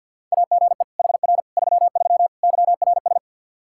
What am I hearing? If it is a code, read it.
Morse code, heard as a keyed tone, about 47 wpm, 700 Hz. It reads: AGE HR 33 YRS
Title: K1USN SST QSO Practice 50wpm